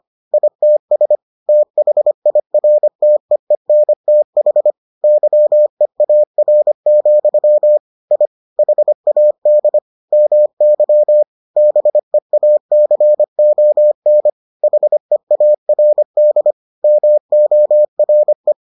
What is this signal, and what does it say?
Morse code, heard as a keyed tone, about 25 wpm, 600 Hz. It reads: ITS THIRTEENTH YEAR, I HAD MY BEACON HEARD MORE